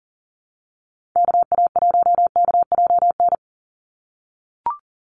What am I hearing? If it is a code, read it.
Morse code, heard as a keyed tone, about 40 wpm, 700 Hz. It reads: KA1KJN
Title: USA Call Signs 40wpm